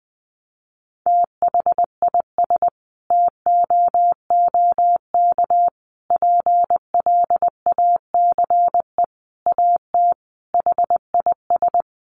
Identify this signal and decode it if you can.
Morse code, heard as a keyed tone, about 20 wpm, 700 Hz. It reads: THIS TOOK PLACE AT HIS
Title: Sentences from Top 300 Words 20wpm